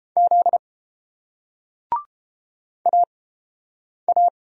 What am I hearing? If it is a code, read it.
Morse code, heard as a keyed tone, about 30 wpm, 700 Hz. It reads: Z A A